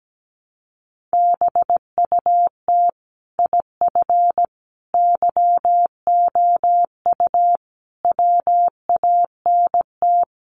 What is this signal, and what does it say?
Morse code, heard as a keyed tone, about 17 wpm, 700 Hz. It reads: BUT IF YOU WANT
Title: Sets of 5 Words Encore2 17wpm